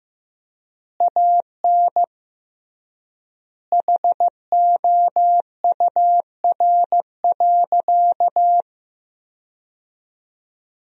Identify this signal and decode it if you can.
Morse code, heard as a keyed tone, about 15 wpm, 700 Hz. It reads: AN HOUR.
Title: 3x Word-Spacing - Sentences from Top 500 Words - 15wpm